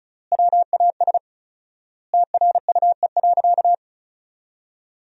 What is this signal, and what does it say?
Morse code, heard as a keyed tone, about 36 wpm, 700 Hz. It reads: WAS TRUE.
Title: 4x Word-Spacing - Sentences from Top 400 Words - 35wpm